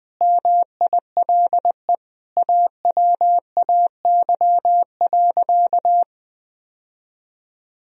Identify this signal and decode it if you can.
Morse code, heard as a keyed tone, about 20 wpm, 700 Hz. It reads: MILE AWAY.
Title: Sentences from Top 1000 Words - Review Part 3 - 20wpm